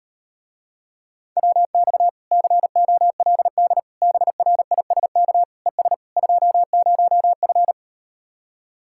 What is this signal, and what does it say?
Morse code, heard as a keyed tone, about 38 wpm, 700 Hz. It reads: WX COLD BRISK ES 20F